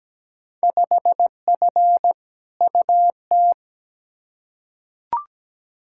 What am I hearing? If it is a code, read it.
Morse code, heard as a keyed tone, about 17 wpm, 700 Hz. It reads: 5F UT
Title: Field Day Exchanges - 17wpm